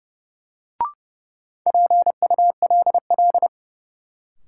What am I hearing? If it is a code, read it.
Morse code, heard as a keyed tone, about 31 wpm, 700 Hz. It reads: PULL
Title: Sentences from Top 400 Words - New Word Drill - 30wpm